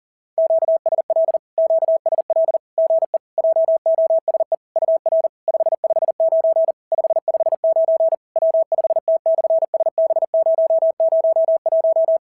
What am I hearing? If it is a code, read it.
Morse code, heard as a keyed tone, about 40 wpm, 650 Hz. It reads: QSL QSL GE JOSE UR 559 559 W5T/SB001
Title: SOTA QSO Practice 40wpm